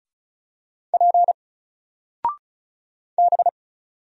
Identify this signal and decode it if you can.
Morse code, heard as a keyed tone, about 36 wpm, 700 Hz. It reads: P B
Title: Single Letter-Number 35wpm